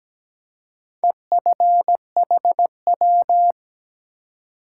Morse code, heard as a keyed tone, about 17 wpm, 700 Hz. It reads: EFHW
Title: Top 180 QSO Elements 17wpm